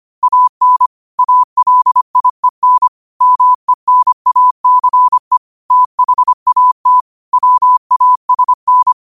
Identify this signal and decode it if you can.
Morse code, heard as a keyed tone, about 25 wpm, 1000 Hz. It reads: AN ALIEN MENACE THAT WASN
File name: qotdfiles / 25wpm